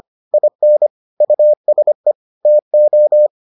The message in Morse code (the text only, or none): IN USE TO